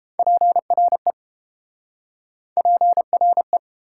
Morse code, heard as a keyed tone, about 29 wpm, 690 Hz. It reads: PRE PRE